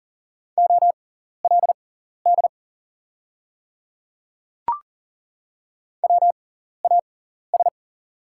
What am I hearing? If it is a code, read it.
Morse code, heard as a keyed tone, about 40 wpm, spaced as for 14 wpm, 690 Hz. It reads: OLD WAS
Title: 6x Character Spacing - Top 300 Words - 40wpm